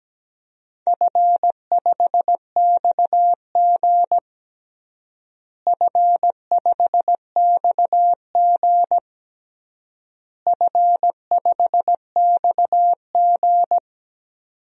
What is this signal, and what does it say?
Morse code, heard as a keyed tone, about 17 wpm, 700 Hz. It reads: F5XG F5XG F5XG
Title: DXCC-100 Call Signs - 4 Characters - Repeated 3x - 17wpm